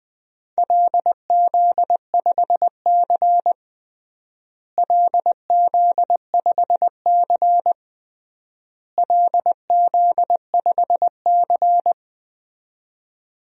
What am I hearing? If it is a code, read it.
Morse code, heard as a keyed tone, about 20 wpm, 700 Hz. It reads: LZ5C LZ5C LZ5C
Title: DXCC-100 Call Signs - 4 Characters - Repeated 3x - 20wpm